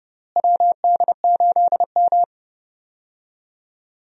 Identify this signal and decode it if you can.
Morse code, heard as a keyed tone, about 30 wpm, 700 Hz. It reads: WD8M
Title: USA Call Signs - 4 Characters - Repeated 3x - 30wpm